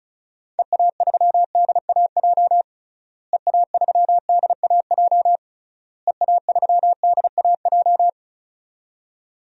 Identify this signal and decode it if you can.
Morse code, heard as a keyed tone, about 36 wpm, 700 Hz. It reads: EA3DAJ EA3DAJ EA3DAJ
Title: DXCC-100 Call Signs - 6 Characters - Repeated 3x - 35wpm